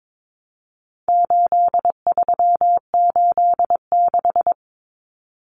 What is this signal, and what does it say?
Morse code, heard as a keyed tone, about 22 wpm, 700 Hz. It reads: 8386
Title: Numbers - 4-Digit 22wpm